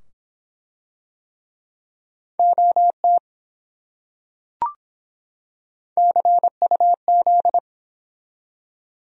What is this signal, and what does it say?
Morse code, heard as a keyed tone, about 25 wpm, 700 Hz. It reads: OT CUZ